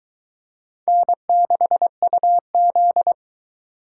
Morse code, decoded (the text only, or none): N6UZ